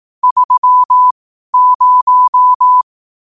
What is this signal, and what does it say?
Morse code, heard as a keyed tone, about 18 wpm, spaced as for 12 wpm, 1000 Hz. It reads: 30